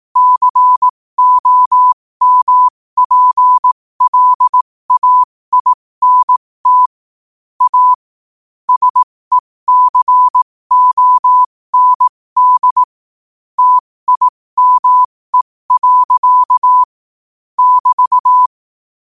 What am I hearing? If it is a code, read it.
Morse code, heard as a keyed tone, about 18 wpm, spaced as for 15 wpm, 1000 Hz. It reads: COMPLAINT A SECOND TIME. <BT>
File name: qotdfiles / 15wpm